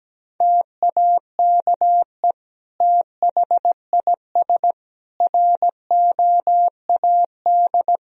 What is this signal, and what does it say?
Morse code, heard as a keyed tone, about 17 wpm, 700 Hz. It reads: TAKE THIS ROAD
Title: Sentences from Top 1000 Words - Review Part 4 - 17wpm